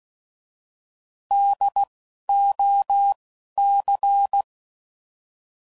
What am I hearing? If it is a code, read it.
Morse code, heard as a keyed tone, about 16 wpm, 800 Hz. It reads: D O C
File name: Prefixes / By First Letter / Ds